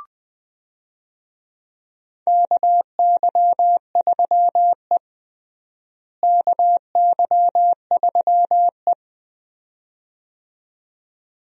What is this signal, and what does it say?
Morse code, heard as a keyed tone, about 21 wpm, 700 Hz. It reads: KY3E KY3E
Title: USA Call Signs - 4 Characters - Repeated 2x - 20wpm